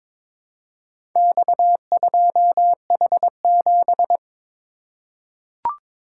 Morse code, heard as a keyed tone, about 21 wpm, 700 Hz. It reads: X2H7
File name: practice / 22wpm / Four Letter-Number Combinations - 22wpm